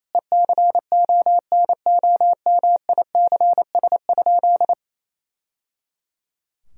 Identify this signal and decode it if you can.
Morse code, heard as a keyed tone, about 28 wpm, 700 Hz. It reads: ECONOMICS?
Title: Sentences from Top 1000 Words - Review Part 3 - 28wpm